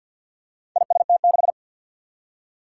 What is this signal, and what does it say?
Morse code, heard as a keyed tone, about 50 wpm, 700 Hz. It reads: IST6
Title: Four Challenging Letter-Number Combinations 50wpm